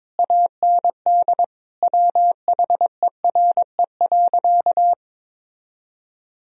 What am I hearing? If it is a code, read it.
Morse code, heard as a keyed tone, about 22 wpm, 700 Hz. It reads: AND WHERE.
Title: Sentences from Top 1000 Words - Review Part 1 - 22wpm